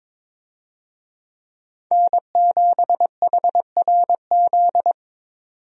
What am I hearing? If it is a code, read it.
Morse code, heard as a keyed tone, about 22 wpm, 700 Hz. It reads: N7HRZ
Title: USA Call Signs - 5 Characters - 22wpm